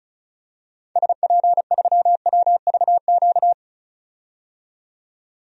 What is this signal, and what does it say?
Morse code, heard as a keyed tone, about 36 wpm, 700 Hz. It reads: SP3WVQ
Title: DXCC-100 Call Signs - 6 Characters - 35wpm